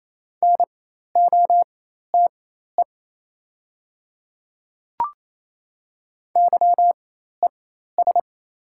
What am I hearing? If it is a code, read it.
Morse code, heard as a keyed tone, about 28 wpm, spaced as for 13 wpm, 690 Hz. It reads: NOTE YES
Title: 4x Character Spacing - Top 500 Words - 28wpm